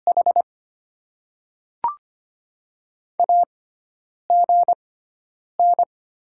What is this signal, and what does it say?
Morse code, heard as a keyed tone, about 25 wpm, spaced as for 9 wpm, 700 Hz. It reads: H AGN
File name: Top 20 QSO Elements - 6x Character Spacing - 25wpm